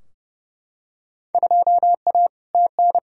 Morse code, heard as a keyed tone, about 30 wpm, 700 Hz. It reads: 2A TN
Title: Field Day Exchanges Encore - 30wpm